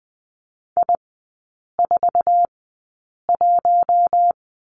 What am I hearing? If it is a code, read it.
Morse code, heard as a keyed tone, about 20 wpm, spaced as for 8 wpm, 700 Hz. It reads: I41
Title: 047 Lesson 9e - Number 4 - Three Characters 20wpm